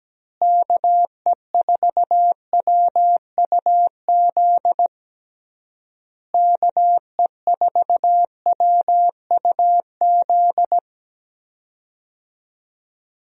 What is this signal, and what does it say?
Morse code, heard as a keyed tone, about 17 wpm, 700 Hz. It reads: KE4WUZ KE4WUZ
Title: USA Call Signs - 6 Characters - Repeated 3x - 17wpm